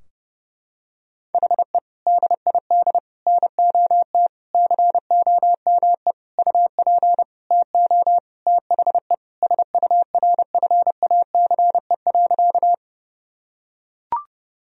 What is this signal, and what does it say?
Morse code, heard as a keyed tone, about 30 wpm, 700 Hz. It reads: HE DID NOT COME UP TO THE SURFACE.
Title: Sentences from Top 1000 Words - Review Part 3 - 30wpm